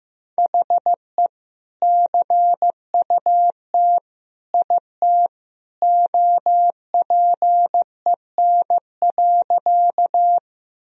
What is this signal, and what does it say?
Morse code, heard as a keyed tone, about 15 wpm, 700 Hz. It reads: HE CUT IT OPEN.
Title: Sentences from Top 1000 Words - Review Part 1 - 15wpm